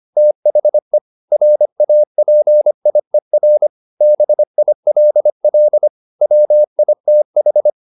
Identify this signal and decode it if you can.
Morse code, heard as a keyed tone, about 25 wpm, 600 Hz. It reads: THE RAPIER BILL WITH
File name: the old man and the sea part 1 25wpm